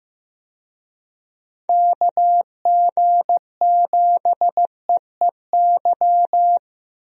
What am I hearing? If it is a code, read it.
Morse code, heard as a keyed tone, about 15 wpm, 700 Hz. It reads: KG7EEY
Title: USA Call Signs - 6 Characters - 15wpm